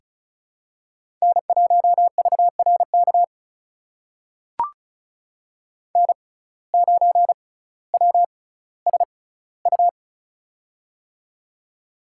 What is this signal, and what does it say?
Morse code, heard as a keyed tone, about 36 wpm, spaced as for 18 wpm, 700 Hz. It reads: N1VRK N9WSU
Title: USA Call Signs - 6x Character Spacing - 35wpm